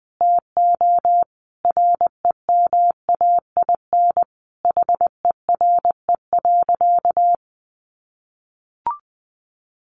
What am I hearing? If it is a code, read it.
Morse code, heard as a keyed tone, about 20 wpm, 700 Hz. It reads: TO REMAIN HERE.
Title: Sentences from Top 1200 Words 20wpm